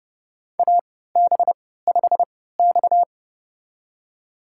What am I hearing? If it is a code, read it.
Morse code, heard as a keyed tone, about 32 wpm, spaced as for 17 wpm, 700 Hz. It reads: AB5X
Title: USA Call Signs - 4 Characters - 3x Character Spacing - 30wpm